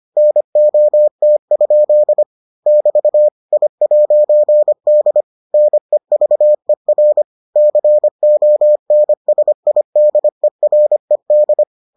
Morse code, heard as a keyed tone, about 25 wpm, 600 Hz. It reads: NOT? <BT> I'D NEVER CONSIDERED